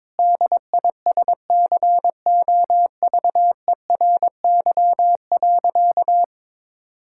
Morse code, heard as a keyed tone, about 22 wpm, 700 Hz. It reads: DISCOVERY.
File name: practice / 22wpm / Sentences with 8-Word Phrases - 22wpm